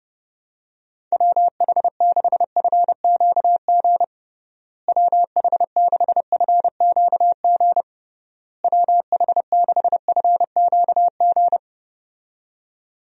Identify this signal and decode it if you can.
Morse code, heard as a keyed tone, about 30 wpm, 700 Hz. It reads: WH6FQG WH6FQG WH6FQG
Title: USA Call Signs - 6 Characters - Repeated 3x - 30wpm